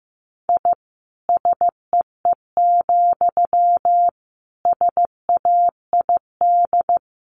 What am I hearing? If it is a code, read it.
Morse code, heard as a keyed tone, about 15 wpm, 700 Hz. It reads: I SEE, SAID
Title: A Sound Of Thunder 15wpm